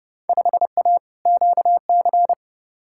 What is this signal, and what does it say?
Morse code, heard as a keyed tone, about 30 wpm, 700 Hz. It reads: 5A QC
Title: Field Day Exchanges Encore - 30wpm